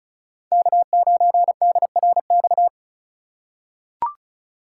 Morse code, heard as a keyed tone, about 34 wpm, 700 Hz. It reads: K9DRX
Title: USA Call Signs - 5 Characters - 3x Character Spacing - 35wpm